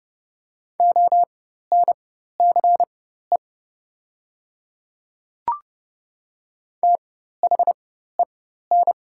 Morse code, heard as a keyed tone, about 30 wpm, spaced as for 14 wpm, 700 Hz. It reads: ONCE THEN
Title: 4x Character Spacing - Top 300 Words - 30wpm